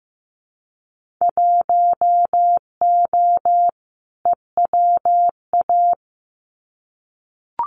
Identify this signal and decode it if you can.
Morse code, heard as a keyed tone, about 15 wpm, 700 Hz. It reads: 1O EWA
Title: Winter Field Day Exchanges Encore 15wpm